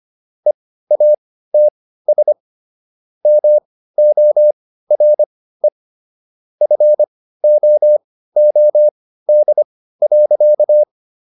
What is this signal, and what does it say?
Morse code, heard as a keyed tone, about 25 wpm, spaced as for 15 wpm, 600 Hz. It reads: EATS MORE FOOD.